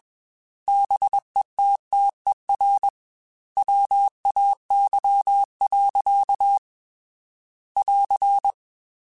Morse code, heard as a keyed tone, about 21 wpm, 800 Hz. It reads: BETTER WAY. <AR>
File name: Technical Subjects / Technical Subjects-mp3s / Gates Intro @20